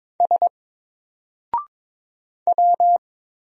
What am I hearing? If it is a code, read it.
Morse code, heard as a keyed tone, about 22 wpm, 700 Hz. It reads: S W
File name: Single-Letter-Number-22wpm